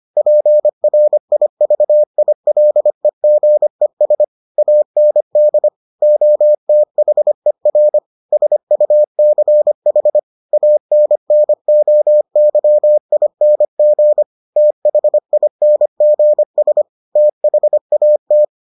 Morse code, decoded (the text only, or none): PRIVILEGES AND OTHER SUCH ANNOYING THINGS THAT